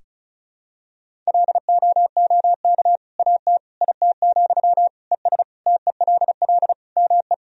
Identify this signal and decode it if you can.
Morse code, heard as a keyed tone, about 34 wpm, 700 Hz. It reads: LOOK AT IT, ES TELL ME
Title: Sentences from Top 200 Words and Abbreviations 35wpm